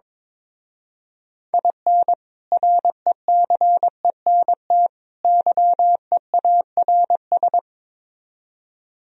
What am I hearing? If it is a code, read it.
Morse code, heard as a keyed tone, about 22 wpm, 700 Hz. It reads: IN RECENT YEARS
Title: Speed-Racing - Sets of 3 Words 22wpm